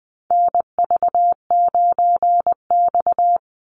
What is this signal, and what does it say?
Morse code, heard as a keyed tone, about 20 wpm, 700 Hz. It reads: NV9X
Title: Call Signs Repeated 2x - 20wpm